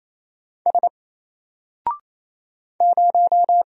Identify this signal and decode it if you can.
Morse code, heard as a keyed tone, about 27 wpm, 700 Hz. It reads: S 0
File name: Single-Letter-Number-28wpm